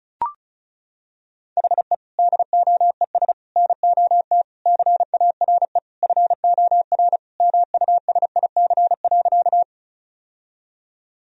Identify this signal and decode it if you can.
Morse code, heard as a keyed tone, about 36 wpm, 700 Hz. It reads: HE DOES NOT CARE FOR MUSIC.
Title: Sentences from Top 1000 Words - Review Part 1 - 35wpm